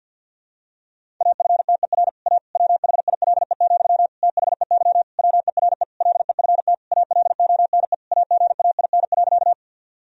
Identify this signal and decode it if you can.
Morse code, heard as a keyed tone, about 48 wpm, 700 Hz. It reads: AFTER A WHILE, THEY WERE LEFT ALONE AGAIN.
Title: Sentences from Top 1000 Words - Review Part 1 - 50wpm